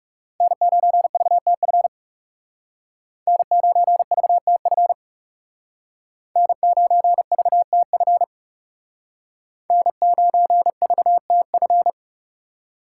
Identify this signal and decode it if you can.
Morse code, heard as a keyed tone, about 39 wpm, 700 Hz. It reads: N9VTF N9VTF N9VTF N9VTF